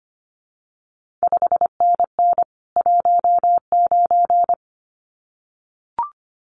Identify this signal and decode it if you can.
Morse code, heard as a keyed tone, about 25 wpm, 700 Hz. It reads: 5NN 19
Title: CQ WW DX Contest Exchange Encore - 25wpm